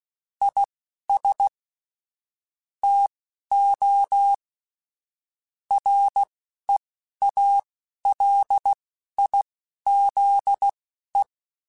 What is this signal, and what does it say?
Morse code, heard as a keyed tone, about 16 wpm, spaced as for 11 wpm, 800 Hz. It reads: IS TO REALIZE
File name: Technical Subjects / Technical Subjects-mp3s / Sw Pwr Sup @10